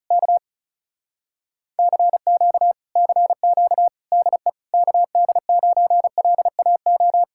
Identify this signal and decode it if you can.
Morse code, heard as a keyed tone, about 40 wpm, 700 Hz. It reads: K CQ CQ DE KD9LAO